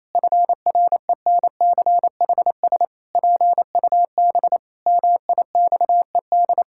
Morse code, heard as a keyed tone, about 28 wpm, 700 Hz. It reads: FRENCHS PUB MIXED
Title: Pangram Challenge v2 - 28wpm